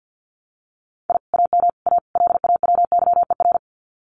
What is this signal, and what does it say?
Morse code, heard as a keyed tone, about 50 wpm, 700 Hz. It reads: I AM A LAWYER
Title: Common QSO Phrases Encore 50wpm